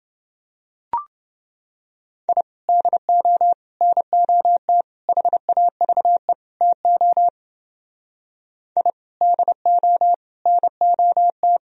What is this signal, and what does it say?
Morse code, heard as a keyed tone, about 29 wpm, 700 Hz. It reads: I DO NOT HAVE TO I DO NOT